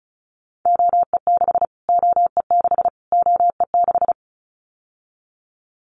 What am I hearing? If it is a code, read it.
Morse code, heard as a keyed tone, about 34 wpm, 700 Hz. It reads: OE6 OE6 OE6
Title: Callsign Prefixes for Top 25 DXCC - Repeated 3x - 35wpm